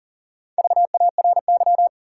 Morse code, heard as a keyed tone, about 39 wpm, 700 Hz. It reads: VARY
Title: Sentences from Top 1200 Words - New Word Drill - 40wpm